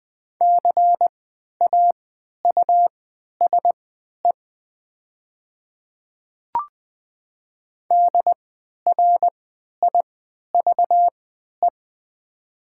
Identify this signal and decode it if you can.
Morse code, heard as a keyed tone, about 20 wpm, spaced as for 11 wpm, 700 Hz. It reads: CAUSE DRIVE
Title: 3x Character Spacing - Top 500 Words - 20wpm